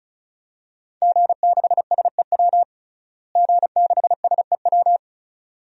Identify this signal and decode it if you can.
Morse code, heard as a keyed tone, about 36 wpm, 700 Hz. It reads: G6SEW G6SEW
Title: DXCC-100 Call Signs - 5 Characters - Repeated 2x - 35wpm